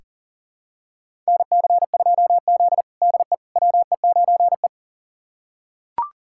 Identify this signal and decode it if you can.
Morse code, heard as a keyed tone, about 37 wpm, 700 Hz. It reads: NC2Z DE WE9E
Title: USA Callsign DE Callsign - 4 Characters - 40wpm